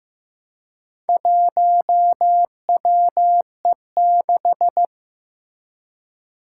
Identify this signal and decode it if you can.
Morse code, heard as a keyed tone, about 15 wpm, 700 Hz. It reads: 1WE6
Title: Four Challenging Letter-Number Combinations 15wpm